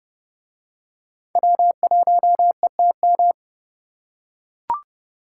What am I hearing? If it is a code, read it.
Morse code, heard as a keyed tone, about 29 wpm, 700 Hz. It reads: W1ETM